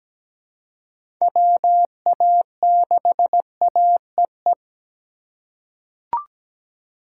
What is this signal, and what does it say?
Morse code, heard as a keyed tone, about 17 wpm, 700 Hz. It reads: WA6AEE
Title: K1USN SST QSO Practice - 17wpm